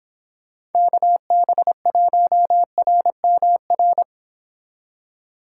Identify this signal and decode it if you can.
Morse code, heard as a keyed tone, about 32 wpm, 700 Hz. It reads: KB1RMR